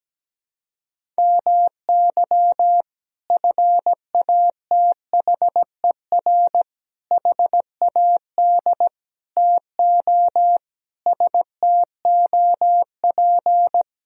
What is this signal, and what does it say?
Morse code, heard as a keyed tone, about 17 wpm, 700 Hz. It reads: MY FATHER HAD TO STOP
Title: Sentences from Top 1000 Words - Review Part 1 - 17wpm